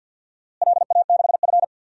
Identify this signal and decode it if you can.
Morse code, heard as a keyed tone, about 47 wpm, 700 Hz. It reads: LA6F
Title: DXCC-100 Call Signs - 4 Characters - Repeated 3x - 50wpm